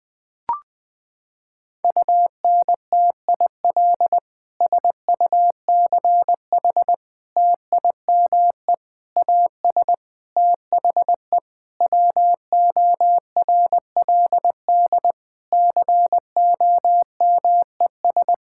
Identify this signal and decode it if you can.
Morse code, heard as a keyed tone, about 21 wpm, 700 Hz. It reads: UNTIL SUCH TIME AS THE WORLD COMES
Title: Sets of 9 Words - 20wpm